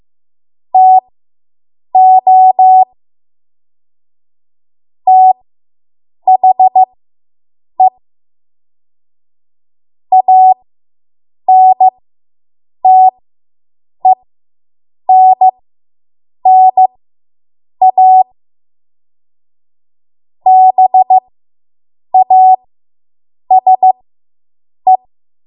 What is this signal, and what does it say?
Morse code, heard as a keyed tone, about 15 wpm, spaced as for 7 wpm, 750 Hz. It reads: TO THE ANTENNA BASE